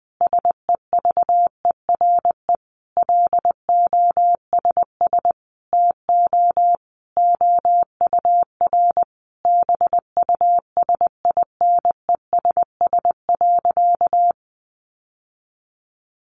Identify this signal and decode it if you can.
Morse code, heard as a keyed tone, about 20 wpm, 700 Hz. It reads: SEVERE LOSS TO OUR BUSINESS.
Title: Sentences from Top 1400 Words 20wpm